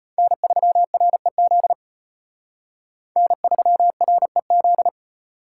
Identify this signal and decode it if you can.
Morse code, heard as a keyed tone, about 33 wpm, 700 Hz. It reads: N3REZ N3REZ